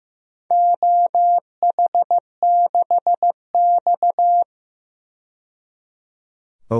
Morse code, heard as a keyed tone, about 16 wpm, 700 Hz. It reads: OH6X
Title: DXCC-100 Call Signs - 4 Characters - 15wpm